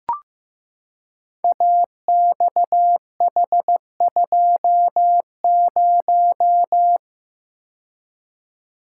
Morse code, heard as a keyed tone, about 15 wpm, 700 Hz. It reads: AXH20
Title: Code Groups - 5 Letter:Numbers - 15wpm